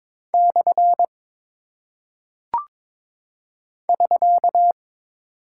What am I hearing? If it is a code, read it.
Morse code, heard as a keyed tone, about 22 wpm, 700 Hz. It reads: / <SK>
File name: practice / 22wpm / Single Punctuation-Prosign - 22wpm